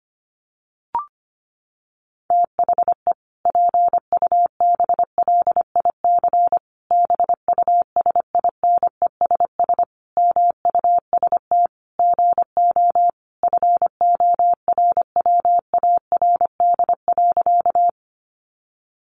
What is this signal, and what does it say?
Morse code, heard as a keyed tone, about 25 wpm, 700 Hz. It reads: THE PUBLIC BUSINESS MUST GO FORWARD.
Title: Sentences from Top 1300 Words 25wpm